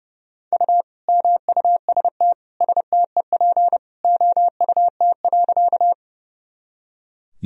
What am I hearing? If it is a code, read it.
Morse code, heard as a keyed tone, about 30 wpm, 700 Hz. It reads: U MUST STEP OUT.
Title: Sentences from Top 400 Words and Abbreviations 30wpm